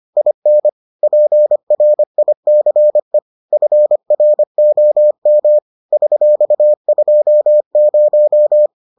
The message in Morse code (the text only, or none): IN PRICE FROM $20